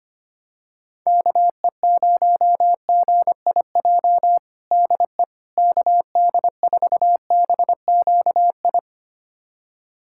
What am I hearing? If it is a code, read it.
Morse code, heard as a keyed tone, about 25 wpm, 700 Hz. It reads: KE0GIJ DE KD4BQI
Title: USA Callsign DE Callsign - 6 Characters - 25wpm